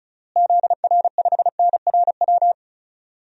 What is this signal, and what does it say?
Morse code, heard as a keyed tone, about 34 wpm, 700 Hz. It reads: ZR5NRW
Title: DXCC-100 Call Signs - 6 Characters - Repeated 3x - 35wpm